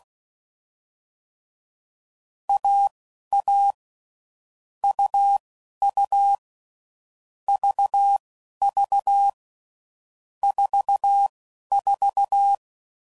Morse code, heard as a keyed tone, about 16 wpm, spaced as for 11 wpm, 800 Hz. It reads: AA UU VV 44
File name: Combinations / Difficult Combos / AUV AUV4